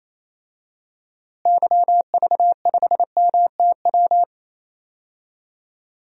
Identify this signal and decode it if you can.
Morse code, heard as a keyed tone, about 27 wpm, 700 Hz. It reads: YV5MTW
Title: DXCC-100 Call Signs - 6 Characters - Repeated 2x - 28wpm